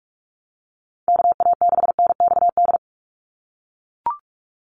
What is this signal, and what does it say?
Morse code, heard as a keyed tone, about 44 wpm, 700 Hz. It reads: KA6NXD
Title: USA Call Signs 45wpm